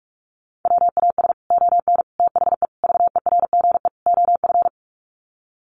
Morse code, heard as a keyed tone, about 44 wpm, 700 Hz. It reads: WAS ON THE VERGE OF
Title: Sets of 5 Words Encore 45wpm